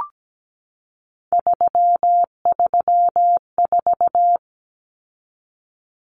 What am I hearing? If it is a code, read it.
Morse code, heard as a keyed tone, about 17 wpm, 700 Hz. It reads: 334
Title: Numbers - 3-Digit 17wpm